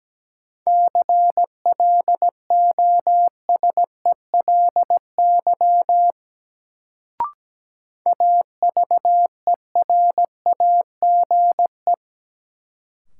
Morse code, heard as a keyed tone, about 17 wpm, 700 Hz. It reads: CLOSELY AVERAGE
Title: Sentences from Top 1400 Words - New Word Drill Alt - 17wpm